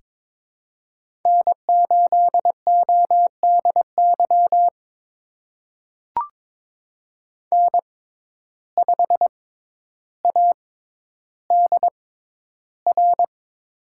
Morse code, heard as a keyed tone, about 21 wpm, spaced as for 11 wpm, 700 Hz. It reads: N8ODY N5ADR
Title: USA Call Signs - 5 Characters - 6x Character Spacing - 22wpm